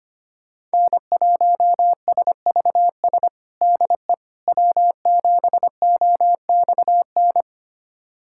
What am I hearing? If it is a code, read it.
Morse code, heard as a keyed tone, about 25 wpm, 700 Hz. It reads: N1SVS DE W7OXN
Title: USA Callsign DE Callsign - 5 Characters - 25wpm